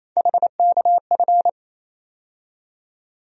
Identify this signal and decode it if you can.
Morse code, heard as a keyed tone, about 27 wpm, 700 Hz. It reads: HKF
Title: USA Call Sign Postfixes - 3 Letters 28wpm